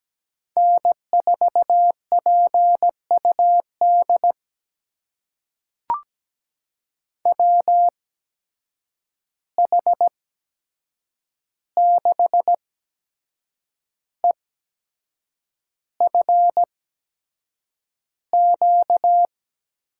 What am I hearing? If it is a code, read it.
Morse code, heard as a keyed tone, about 17 wpm, spaced as for 7 wpm, 700 Hz. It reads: N4PUD WH6EFQ
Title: USA Call Signs - 8x Character Spacing - 17wpm